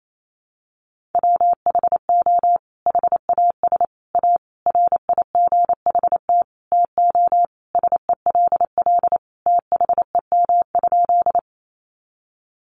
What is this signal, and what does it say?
Morse code, heard as a keyed tone, about 28 wpm, 700 Hz. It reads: WHO HAS A RIGHT TO SELL THEM?
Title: Sentences from Top 1100 Words 28wpm